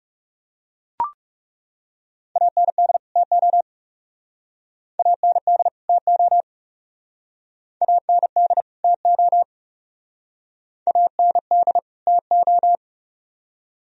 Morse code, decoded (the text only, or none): AND TO AND TO AND TO AND TO